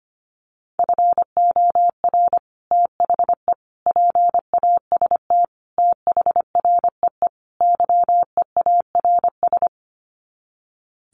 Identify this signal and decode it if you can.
Morse code, heard as a keyed tone, about 25 wpm, 700 Hz. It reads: FOR THE PAST THREE YEARS
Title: Sets of 5 Words Encore2 25wpm